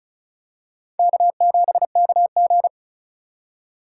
Morse code, heard as a keyed tone, about 36 wpm, 700 Hz. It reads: K7KG
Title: USA Call Signs - 4 Characters - 3x Character Spacing - 35wpm